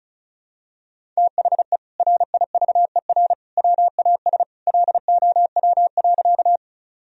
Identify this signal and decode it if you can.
Morse code, heard as a keyed tone, about 36 wpm, 700 Hz. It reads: THE RIVER WAS LOW.
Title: Sentences from Top 1000 Words - Review Part 1 - 35wpm